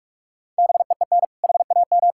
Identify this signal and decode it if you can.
Morse code, heard as a keyed tone, about 46 wpm, 700 Hz. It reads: BEEN HAM
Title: Practice QSO - Exchange 3 - More Info - 45wpm